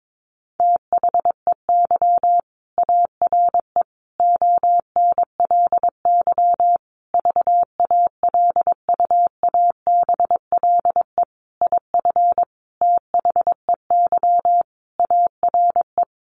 Morse code, heard as a keyed tone, about 22 wpm, 700 Hz. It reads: THEY ARE ONLY VALUABLE IF THEY ARE
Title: Sentences from Top 1400 Words 22wpm